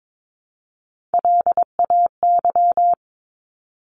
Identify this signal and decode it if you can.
Morse code, heard as a keyed tone, about 22 wpm, 700 Hz. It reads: LAY
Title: Top 500 Words 22wpm